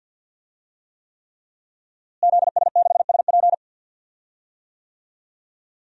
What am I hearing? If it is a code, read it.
Morse code, heard as a keyed tone, about 51 wpm, 700 Hz. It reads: ZS6SP
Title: DXCC-100 Call Signs - 5 Characters - Repeated 3x - 50wpm